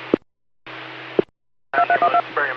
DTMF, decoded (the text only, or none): 2312